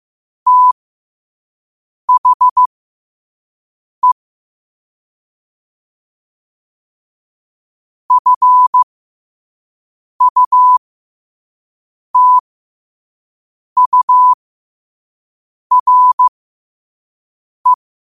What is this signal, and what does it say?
Morse code, heard as a keyed tone, about 15 wpm, spaced as for 5 wpm, 1000 Hz. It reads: THE FUTURE
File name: qotdfiles / 5wpm